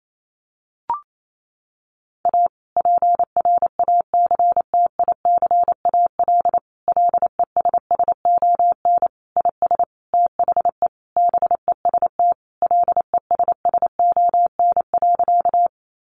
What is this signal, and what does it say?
Morse code, heard as a keyed tone, about 28 wpm, 700 Hz. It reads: A PRACTICAL LESSON IS THE BEST LESSON.
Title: Sentences from Top 1400 Words 28wpm